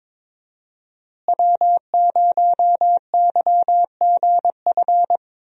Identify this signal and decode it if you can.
Morse code, heard as a keyed tone, about 21 wpm, 700 Hz. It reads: W0YGF
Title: USA Call Signs - 5 Characters - Repeated 2x - 22wpm